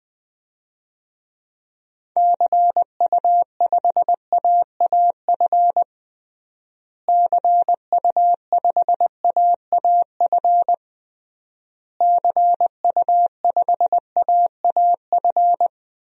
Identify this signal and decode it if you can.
Morse code, heard as a keyed tone, about 19 wpm, 700 Hz. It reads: CU5AAF CU5AAF CU5AAF
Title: DXCC-100 Call Signs - 6 Characters - Repeated 3x - 20wpm